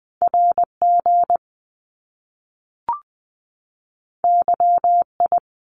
Morse code, heard as a keyed tone, about 20 wpm, 700 Hz. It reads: RG YI
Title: Top Two Letter Combinations in Words 20wpm